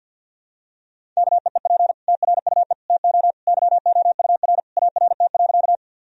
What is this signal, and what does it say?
Morse code, heard as a keyed tone, about 52 wpm, 700 Hz. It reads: KEEP TRUE TO YOUR ART.
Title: Sentences from Top 1000 Words - Review Part 3 - 50wpm